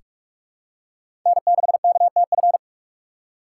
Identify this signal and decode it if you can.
Morse code, heard as a keyed tone, about 41 wpm, 700 Hz. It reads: N6KTF
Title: USA Call Signs - 5 Characters - 2x Character Spacing - 45wpm